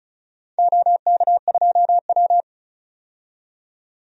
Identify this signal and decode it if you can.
Morse code, heard as a keyed tone, about 34 wpm, 700 Hz. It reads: OK2W
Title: DXCC-100 Call Signs - 4 Characters - 35wpm